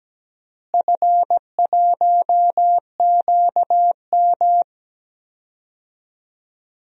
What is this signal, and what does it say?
Morse code, heard as a keyed tone, about 17 wpm, 700 Hz. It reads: F1QM
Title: DXCC-100 Call Signs - 4 Characters - Repeated 2x - 17wpm